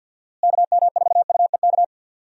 Morse code, heard as a keyed tone, about 47 wpm, 700 Hz. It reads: KM4UEK
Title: USA Call Signs - 6 Characters - Repeated 3x - 50wpm